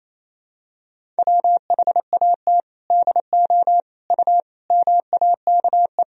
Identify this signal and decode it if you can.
Morse code, heard as a keyed tone, about 28 wpm, 700 Hz. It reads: WHAT DO U MAKE
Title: Sentences from Top 100 Words and Abbreviations 28wpm